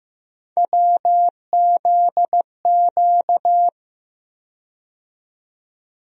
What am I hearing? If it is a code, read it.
Morse code, heard as a keyed tone, about 15 wpm, 700 Hz. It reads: WZQ
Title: USA Call Sign Postfixes - 3 Letters 15wpm